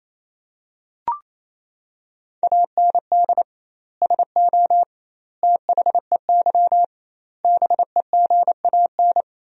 Morse code, heard as a keyed tone, about 28 wpm, 700 Hz. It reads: AND SO THEY BEGAN
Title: 2x Word-Spacing - Sentences from Top 400 Words - 28wpm